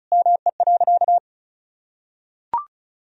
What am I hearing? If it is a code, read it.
Morse code, heard as a keyed tone, about 36 wpm, 700 Hz. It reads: ME.
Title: Sentences from Top 1000 Words - Review Part 4 - 35wpm